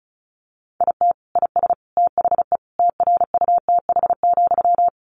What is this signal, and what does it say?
Morse code, heard as a keyed tone, about 34 wpm, 700 Hz. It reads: IT IS THE TRUTH,
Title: Sentences from Top 800 Words 35wpm